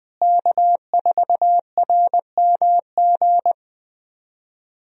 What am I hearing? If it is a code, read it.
Morse code, heard as a keyed tone, about 20 wpm, 700 Hz. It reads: K4RMG
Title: USA Call Signs - 5 Characters - Repeated 3x - 20wpm